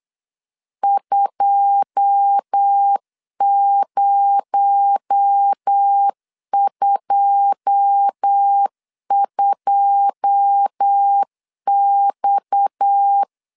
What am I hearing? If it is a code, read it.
Morse code, heard as a keyed tone, about 8 wpm, 800 Hz. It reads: 2022X
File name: LEZIONI / 9 2 8wpm 3437